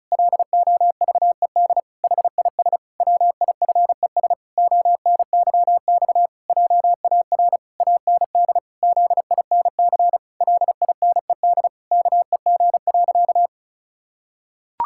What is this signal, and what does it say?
Morse code, heard as a keyed tone, about 34 wpm, 700 Hz. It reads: LOVED HIS WIFES ONYX JAR AND ZINC LINED KEG.
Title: Pangram Challenge v2 - 35wpm